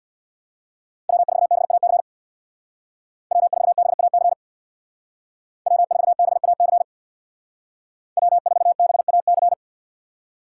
Morse code, heard as a keyed tone, about 61 wpm, 700 Hz. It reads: W4BAC W4BAC W4BAC W4BAC
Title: Speed Racing - USA Call Signs 50wpm